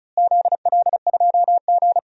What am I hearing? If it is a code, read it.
Morse code, heard as a keyed tone, about 36 wpm, 700 Hz. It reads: ZL2G
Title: DXCC-100 Call Signs - 4 Characters - Repeated 2x - 35wpm